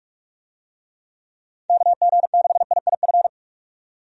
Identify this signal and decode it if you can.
Morse code, heard as a keyed tone, about 46 wpm, 700 Hz. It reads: KG6IIF
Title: USA Call Signs - 6 Characters - 45wpm